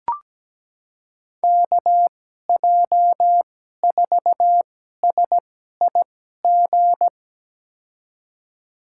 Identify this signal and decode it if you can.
Morse code, heard as a keyed tone, about 17 wpm, spaced as for 12 wpm, 700 Hz. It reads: KJ4SIG
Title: USA Call Signs - 6 Characters - 2x Character Spacing - 17wpm